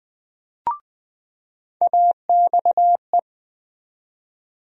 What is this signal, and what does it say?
Morse code, heard as a keyed tone, about 19 wpm, 700 Hz. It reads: AXE
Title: USA Call Sign Postfixes - 3 Letters 20wpm